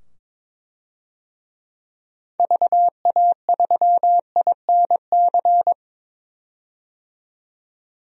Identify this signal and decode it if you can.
Morse code, heard as a keyed tone, about 22 wpm, 700 Hz. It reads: VA3INC
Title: DXCC-100 Call Signs - 6 Characters - 22wpm